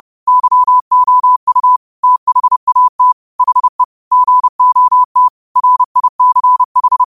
Morse code, heard as a keyed tone, about 30 wpm, 1000 Hz. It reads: YOU THAT HE GOT RICH
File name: qotdfiles / 30wpm